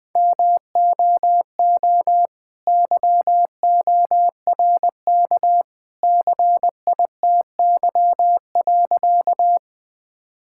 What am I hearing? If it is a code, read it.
Morse code, heard as a keyed tone, about 21 wpm, 700 Hz. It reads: MOO YORK CITY.
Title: All-Age Appropriate Jokes - 20wpm